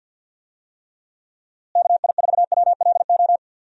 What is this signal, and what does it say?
Morse code, heard as a keyed tone, about 50 wpm, 700 Hz. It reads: KI4WLO
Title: USA Call Signs - 6 Characters - Repeated 2x - 50wpm